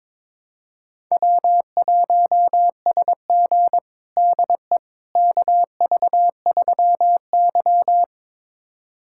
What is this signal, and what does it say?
Morse code, heard as a keyed tone, about 22 wpm, 700 Hz. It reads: W1SG DE KV3Y
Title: USA Callsign DE Callsign - 4 Characters - 22wpm